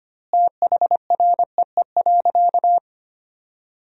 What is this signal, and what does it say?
Morse code, heard as a keyed tone, about 25 wpm, 700 Hz. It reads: THREE.
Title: 2x Word-Spacing - Sentences from Top 200 Words - 25wpm